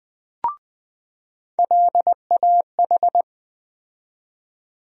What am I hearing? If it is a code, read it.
Morse code, heard as a keyed tone, about 20 wpm, 700 Hz. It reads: LAH
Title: USA Call Sign Postfixes - 3 Letters 20wpm